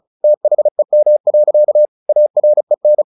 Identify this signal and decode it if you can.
Morse code, heard as a keyed tone, about 34 wpm, 600 Hz. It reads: THEM. AREN
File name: the old man and the sea / the old man and the sea 1 35wpm